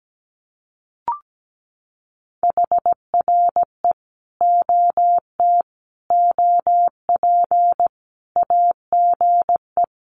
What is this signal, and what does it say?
Morse code, heard as a keyed tone, about 17 wpm, 700 Hz. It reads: HRE OT OP AGE
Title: Common QSO Phrases 17wpm